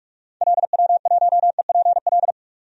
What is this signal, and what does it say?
Morse code, heard as a keyed tone, about 44 wpm, 700 Hz. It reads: LW1EPL
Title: DXCC-100 Call Signs - 6 Characters - 45wpm